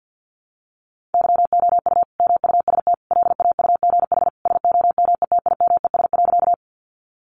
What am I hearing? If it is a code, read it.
Morse code, heard as a keyed tone, about 50 wpm, 700 Hz. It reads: YOU MUST LAUGH SOMETIMES.
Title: Sentences from Top 600 Words 50wpm